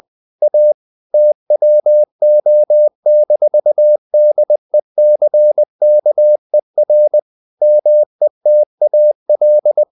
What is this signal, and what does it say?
Morse code, heard as a keyed tone, about 20 wpm, 600 Hz. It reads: A TWO-DECKER METAL
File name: the old man and the sea / the old man and the sea part 1 20wpm